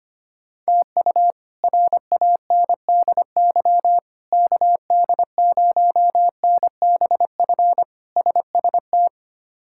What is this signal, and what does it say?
Morse code, heard as a keyed tone, about 25 wpm, 700 Hz. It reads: TU RANDY KD0NBF SST
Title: K1USN SST QSO Practice - 25wpm